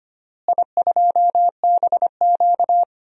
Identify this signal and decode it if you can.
Morse code, heard as a keyed tone, about 25 wpm, 700 Hz. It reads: I2BQ
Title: DXCC-100 Call Signs - 4 Characters - Repeated 3x - 25wpm